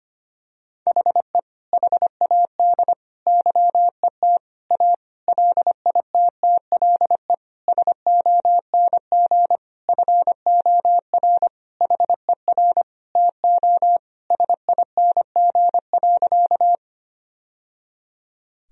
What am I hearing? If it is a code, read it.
Morse code, heard as a keyed tone, about 25 wpm, 700 Hz. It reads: HE HAD YET A LITTLE SONG FOR HER TO SING.
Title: Sentences from Top 1000 Words - Review Part 2 - 25wpm